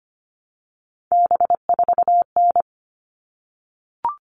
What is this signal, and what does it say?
Morse code, heard as a keyed tone, about 25 wpm, 700 Hz. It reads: B4N
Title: All Cut Numbers - 3-Digit 25wpm